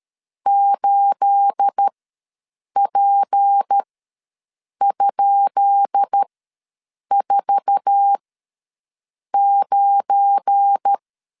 Morse code, heard as a keyed tone, about 13 wpm, spaced as for 7 wpm, 800 Hz. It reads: 8P?49
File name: LEZIONI / 10 SIGNS / 10B INT ws 12wpm 3527